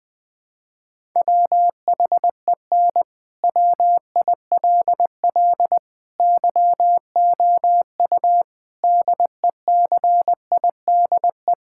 Morse code, heard as a keyed tone, about 21 wpm, 700 Hz. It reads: WHEN WILL YOU DECIDE
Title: Sentences from Top 1000 Words - Review Part 6 - 20wpm